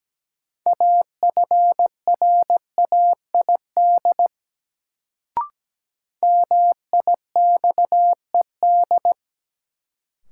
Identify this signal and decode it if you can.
Morse code, heard as a keyed tone, about 17 wpm, 700 Hz. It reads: AFRAID MIXED
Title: Sentences from Top 1500 Words - New Word Drill Alt - 17wpm